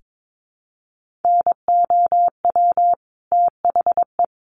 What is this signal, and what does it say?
Morse code, heard as a keyed tone, about 22 wpm, 700 Hz. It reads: NOW THE
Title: Sentences from Top 1500 Words 22wpm